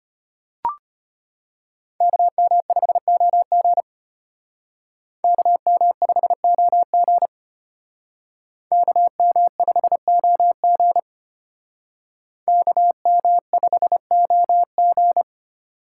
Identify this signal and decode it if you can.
Morse code, heard as a keyed tone, about 33 wpm, 700 Hz. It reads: KM5OG KM5OG KM5OG KM5OG